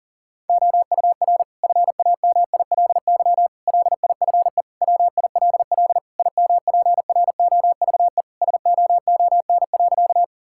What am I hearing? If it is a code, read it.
Morse code, heard as a keyed tone, about 37 wpm, 700 Hz. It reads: OUR FAMILY LIFE WILL IMPROVE SOON.
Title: Sentences from Top 1000 Words - Review Part 5 - 40wpm